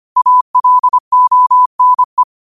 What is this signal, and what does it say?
Morse code, heard as a keyed tone, about 25 wpm, 1000 Hz. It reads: ALONE
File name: qotdfiles / 25wpm